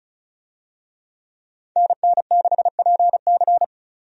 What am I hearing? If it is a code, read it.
Morse code, heard as a keyed tone, about 36 wpm, 700 Hz. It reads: NN6PC
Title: USA Call Signs - 5 Characters - 35wpm